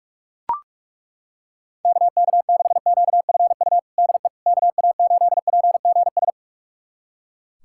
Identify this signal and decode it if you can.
Morse code, heard as a keyed tone, about 41 wpm, 700 Hz. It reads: KK6QFU DE KA8PGS
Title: USA Callsign DE Callsign - 6 Characters - 45wpm